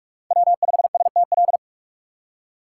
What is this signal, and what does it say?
Morse code, heard as a keyed tone, about 44 wpm, 700 Hz. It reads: W 5 S T L
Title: USA Call Signs - 5 Characters - 6x Character Spacing - 45wpm